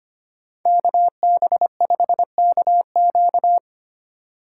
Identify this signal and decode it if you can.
Morse code, heard as a keyed tone, about 30 wpm, 700 Hz. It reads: KB5KQ
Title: Speed Racing - USA 2x2 Call Signs 25wpm